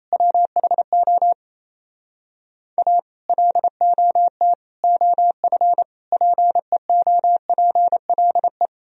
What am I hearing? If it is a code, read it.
Morse code, heard as a keyed tone, about 36 wpm, 700 Hz. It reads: WHO A LOT OF PEOPLE